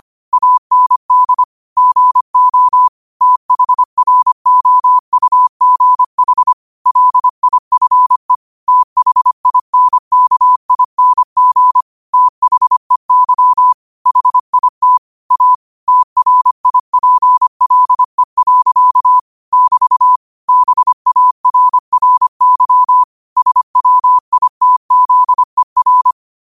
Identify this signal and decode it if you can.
Morse code, heard as a keyed tone, about 25 wpm, 1000 Hz. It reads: AND GO THROUGH LIFE THINKING THEY HIT A TRIPLE. <BT> BARRY SWITZER
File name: qotdfiles / 25wpm